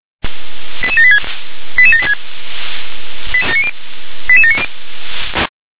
Five-tone selective call (ZVEI), heard as quick runs of five tones, tones of about 70 ms. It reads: 88786 88786 88790